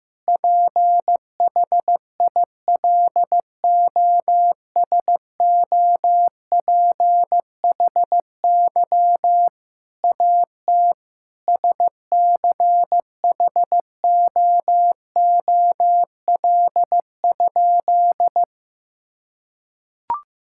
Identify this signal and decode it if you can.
Morse code, heard as a keyed tone, about 15 wpm, 700 Hz. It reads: PHILOSOPHY AT SCHOOL?
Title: Sentences from Top 1000 Words - Review Part 6 - 15wpm